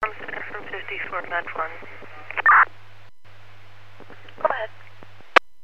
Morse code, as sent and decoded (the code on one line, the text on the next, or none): none